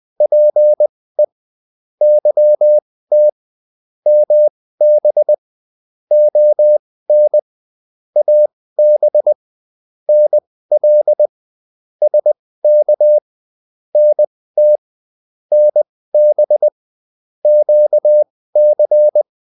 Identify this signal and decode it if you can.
Morse code, heard as a keyed tone, about 20 wpm, spaced as for 15 wpm, 600 Hz. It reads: PE YT MB ON AB NL SK NT NB QC
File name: Provinces-15